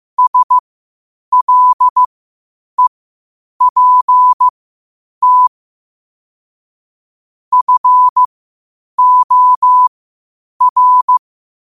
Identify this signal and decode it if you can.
Morse code, heard as a keyed tone, about 15 wpm, spaced as for 8 wpm, 1000 Hz. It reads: SLEPT FOR
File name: qotdfiles / 8wpm